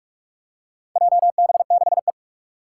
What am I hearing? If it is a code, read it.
Morse code, heard as a keyed tone, about 44 wpm, 700 Hz. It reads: JB6E
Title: Four Challenging Letter-Number Combinations 45wpm